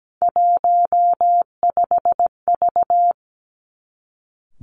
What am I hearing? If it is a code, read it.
Morse code, heard as a keyed tone, about 17 wpm, 700 Hz. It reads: 15V
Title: Three Challenging Letter-Number Combinations 17wpm